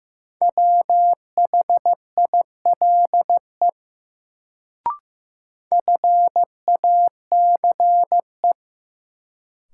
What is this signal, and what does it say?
Morse code, heard as a keyed tone, about 15 wpm, 700 Hz. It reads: WHILE FACE
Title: Sentences from Top 300 Words - New Word Drill - 15wpm